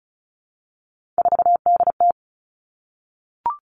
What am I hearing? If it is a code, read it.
Morse code, heard as a keyed tone, about 34 wpm, 700 Hz. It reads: 4DT
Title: All Cut Numbers - 3-Digit 35wpm